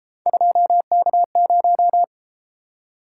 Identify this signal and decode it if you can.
Morse code, heard as a keyed tone, about 31 wpm, 700 Hz. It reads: 2K0